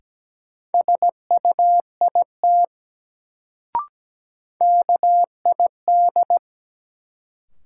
Morse code, heard as a keyed tone, about 17 wpm, 700 Hz. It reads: SUIT KID
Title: Sentences from Top 1600 Words - New Word Drill - 17wpm